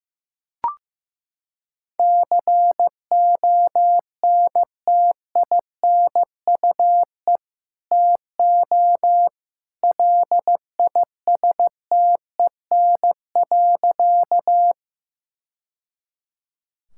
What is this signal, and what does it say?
Morse code, heard as a keyed tone, about 15 wpm, 700 Hz. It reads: CONTINUE TO LISTEN.
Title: Sentences from Top 1000 Words - Review Part 6 - 15wpm